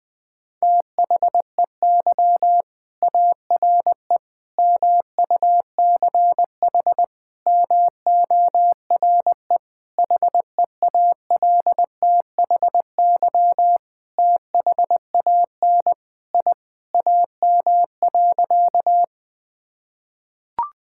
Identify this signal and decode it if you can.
Morse code, heard as a keyed tone, about 20 wpm, 700 Hz. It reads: THEY ARE MUCH MORE HEALTHY THAN I AM.
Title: Sentences from Top 1000 Words - Review Part 5 - 20wpm